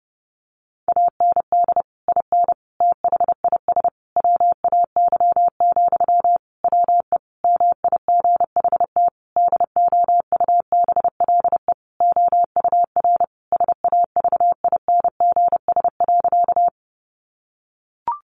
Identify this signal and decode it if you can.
Morse code, heard as a keyed tone, about 30 wpm, 700 Hz. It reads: AND IN THIS WAY, WE MIGHT DOUBLE OUR SAVINGS.
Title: Sentences from Top 1400 Words 30wpm